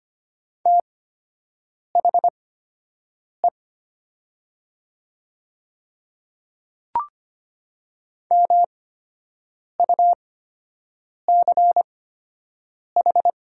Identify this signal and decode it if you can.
Morse code, heard as a keyed tone, about 25 wpm, spaced as for 7 wpm, 700 Hz. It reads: THE MUCH
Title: 8x Character Spacing - Top 200 Words - 25wpm